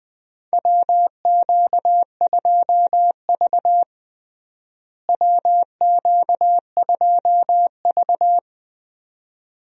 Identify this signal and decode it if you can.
Morse code, heard as a keyed tone, about 21 wpm, 700 Hz. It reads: WQ2V WQ2V
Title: USA Call Signs - 4 Characters - Repeated 2x - 20wpm